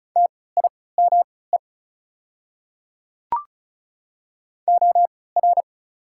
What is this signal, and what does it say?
Morse code, heard as a keyed tone, about 34 wpm, spaced as for 20 wpm, 690 Hz. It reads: TIME OR
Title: 3x Character Spacing - Top 100 Words - 35wpm